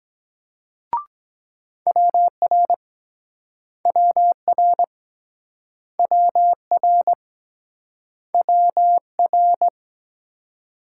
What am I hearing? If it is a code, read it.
Morse code, heard as a keyed tone, about 22 wpm, 700 Hz. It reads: WR WR WR WR